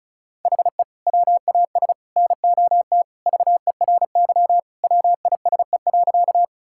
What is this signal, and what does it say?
Morse code, heard as a keyed tone, about 36 wpm, 700 Hz. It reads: HE WAS NOT VERY WISE.
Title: Sentences from Top 1600 Words 35wpm